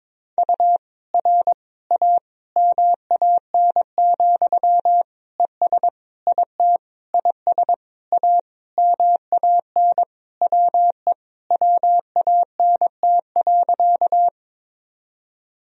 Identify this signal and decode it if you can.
Morse code, heard as a keyed tone, about 22 wpm, 700 Hz. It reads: U R A MAN, ES IT IS A MAN WE WANT.
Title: Sentences from Top 200 Words and Abbreviations 22wpm